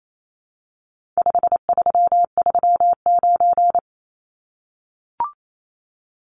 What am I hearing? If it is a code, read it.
Morse code, heard as a keyed tone, about 28 wpm, 700 Hz. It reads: 5339
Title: Numbers - 4-Digit 28wpm